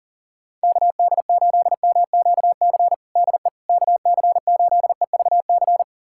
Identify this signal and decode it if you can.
Morse code, heard as a keyed tone, about 40 wpm, 700 Hz. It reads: KD8MQC DE KC8EVC
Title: USA Callsign DE Callsign 40wpm